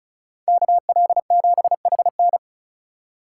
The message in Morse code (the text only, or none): KL7HN